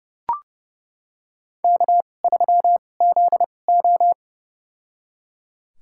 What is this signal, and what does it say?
Morse code, heard as a keyed tone, about 30 wpm, spaced as for 22 wpm, 700 Hz. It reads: K3ZO
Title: USA Call Signs - 4 Characters - 2x Character Spacing - 30wpm